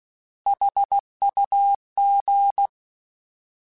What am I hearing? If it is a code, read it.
Morse code, heard as a keyed tone, about 16 wpm, 800 Hz. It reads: HUG
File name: Alphabetical / Alphabetical mp3s / Hs @15